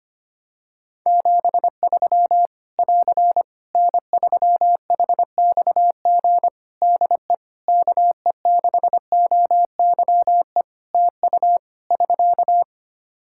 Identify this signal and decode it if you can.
Morse code, heard as a keyed tone, about 25 wpm, 700 Hz. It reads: 73 <AR> N3HXG DE KE6OYE TU <SK>
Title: Practice QSO - Exchange 4 - Ending - 25wpm